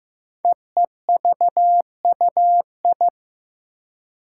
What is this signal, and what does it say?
Morse code, heard as a keyed tone, about 15 wpm, 700 Hz. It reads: EEVUI
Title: Code Groups - 5 Letters - 15wpm